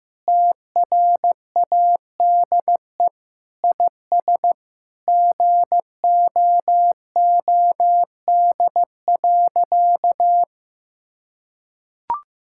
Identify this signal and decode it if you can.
Morse code, heard as a keyed tone, about 15 wpm, 700 Hz. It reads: TRADE IS GOOD.
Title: Sentences from Top 1000 Words - Review Part 6 - 15wpm